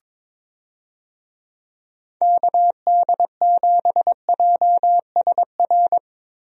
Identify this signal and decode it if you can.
Morse code, heard as a keyed tone, about 22 wpm, 700 Hz. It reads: KD7JSR
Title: USA Call Signs - 6 Characters - Repeated 3x - 22wpm